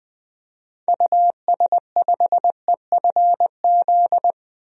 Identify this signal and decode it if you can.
Morse code, heard as a keyed tone, about 20 wpm, 700 Hz. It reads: US5EFZ
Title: DXCC-100 Call Signs - 6 Characters - 20wpm